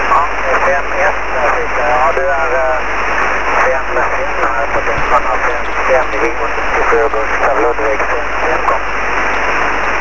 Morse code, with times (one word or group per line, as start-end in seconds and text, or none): none